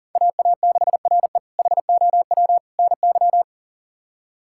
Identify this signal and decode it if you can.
Morse code, heard as a keyed tone, about 40 wpm, 700 Hz. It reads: AA6RE HOW NY
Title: K1USN SST QSO Practice - 40wpm